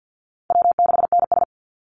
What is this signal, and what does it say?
Morse code, heard as a keyed tone, about 50 wpm, 700 Hz. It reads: W6NS
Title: Easy Call Signs - 50wpm